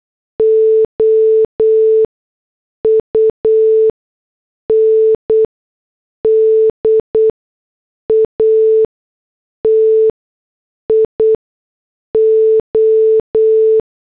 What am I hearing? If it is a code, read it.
Morse code, heard as a keyed tone, about 8 wpm, spaced as for 6 wpm, 440 Hz. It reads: OUNDATIO